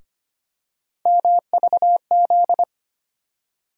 Morse code, heard as a keyed tone, about 25 wpm, 700 Hz. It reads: MVZ
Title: USA Call Sign Postfixes - 3 Letters 25wpm